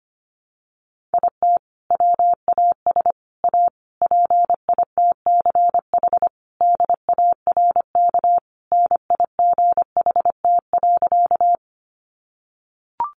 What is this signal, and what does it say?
Morse code, heard as a keyed tone, about 25 wpm, 700 Hz. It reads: IT WAS A PITCH DARK NIGHT.
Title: Sentences from Top 1300 Words 25wpm